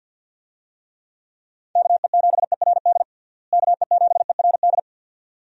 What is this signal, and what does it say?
Morse code, heard as a keyed tone, about 47 wpm, 700 Hz. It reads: KE7ERD KE7ERD
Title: USA Call Signs - 6 Characters - Repeated 2x - 50wpm